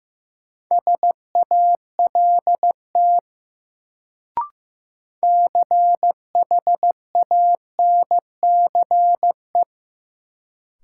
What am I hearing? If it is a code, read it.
Morse code, heard as a keyed tone, about 15 wpm, 700 Hz. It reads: SALT CHANCE
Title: Sentences from Top 800 Words - New Word Drill - 15wpm